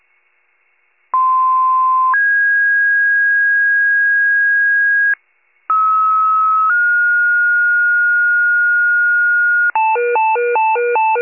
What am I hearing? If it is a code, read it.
Two-tone page, heard as a one-second tone, then a three-second tone, about 1040 then 1650 Hz, about 1290 then 1430 Hz, with noise. It is heard from 1.1 to 9.7 s.
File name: pager / 95 RESCUE DUTY / wcalert